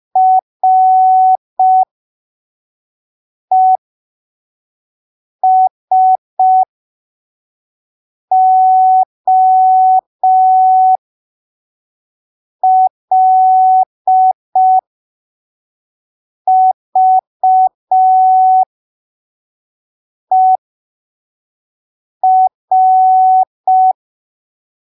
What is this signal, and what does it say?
Morse code, heard as a keyed tone, about 5 wpm, spaced as for 3 wpm, 750 Hz. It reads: RESOLVER
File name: arquivos / texto / texto 02 5ppm